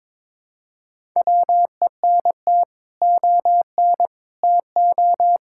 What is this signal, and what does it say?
Morse code, heard as a keyed tone, about 22 wpm, 700 Hz. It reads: WENT ON TO
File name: practice / 22wpm / Sets of 7 Words - 22wpm